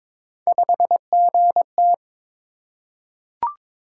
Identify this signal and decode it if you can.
Morse code, heard as a keyed tone, about 22 wpm, 700 Hz. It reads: 5GT
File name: practice / 22wpm / Three Letter-Number Combinations - 22wpm